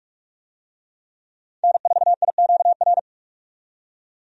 Morse code, heard as a keyed tone, about 41 wpm, 700 Hz. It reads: N4IQR
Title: USA Call Signs - 5 Characters - 45wpm